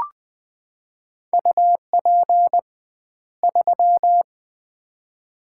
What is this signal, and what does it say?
Morse code, heard as a keyed tone, about 20 wpm, 700 Hz. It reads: UP 3
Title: Common QSO Phrases - 2x Word-Spacing - 20wpm